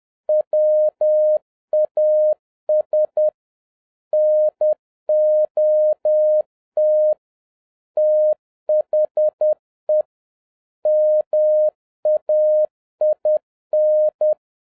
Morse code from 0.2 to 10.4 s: WAS NOT THE